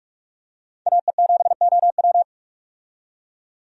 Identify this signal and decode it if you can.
Morse code, heard as a keyed tone, about 46 wpm, 700 Hz. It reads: AE7OW
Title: USA Call Signs - 5 Characters - 45wpm